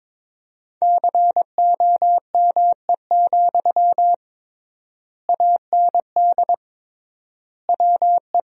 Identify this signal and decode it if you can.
Morse code, heard as a keyed tone, about 21 wpm, 700 Hz. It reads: COME, AND WE
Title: 3x Word-Spacing - Sentences from Top 500 Words - 22wpm